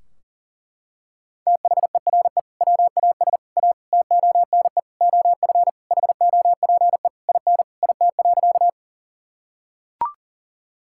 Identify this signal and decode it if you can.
Morse code, heard as a keyed tone, about 40 wpm, 700 Hz. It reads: THERE WAS A TONE OF HOPE IN IT.
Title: Sentences from Top 1600 Words 40wpm